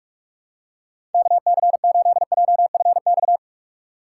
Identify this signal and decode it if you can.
Morse code, heard as a keyed tone, about 42 wpm, 700 Hz. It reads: KC8JFX
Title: USA Call Signs - 6 Characters - 3x Character Spacing - 45wpm